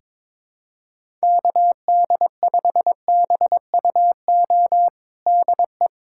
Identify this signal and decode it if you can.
Morse code, heard as a keyed tone, about 22 wpm, 700 Hz. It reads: KD5BUO DE
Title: USA Callsign DE Callsign - 6 Characters - 22wpm